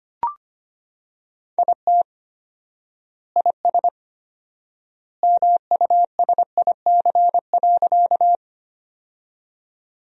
Morse code, heard as a keyed tone, about 25 wpm, 700 Hz. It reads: IT IS MUSIC.
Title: 4x Word-Spacing - Sentences from Top 300 Words - 25wpm